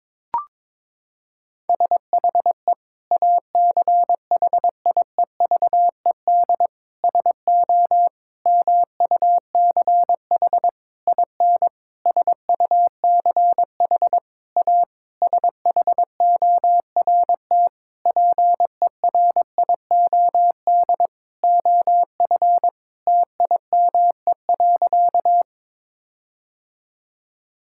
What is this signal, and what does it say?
Morse code, heard as a keyed tone, about 22 wpm, 700 Hz. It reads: SHE ACHIEVED SO MUCH IN SUCH A SHORT PERIOD OF TIME.
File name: Sentences with 7-Word Phrases - 22wpm